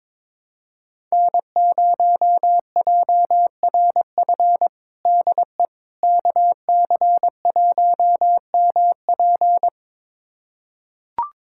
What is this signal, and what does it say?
Morse code, heard as a keyed tone, about 22 wpm, 700 Hz. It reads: N0JRF DE KC1MP
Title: USA Callsign DE Callsign - 5 Characters - 22wpm